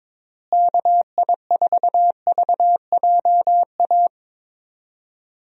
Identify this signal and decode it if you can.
Morse code, heard as a keyed tone, about 21 wpm, 700 Hz. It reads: KI4VJA
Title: USA Call Signs - 6 Characters - Repeated 2x - 22wpm